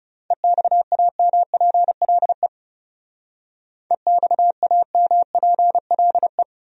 Morse code, heard as a keyed tone, about 39 wpm, 700 Hz. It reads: EXAMPLE EXAMPLE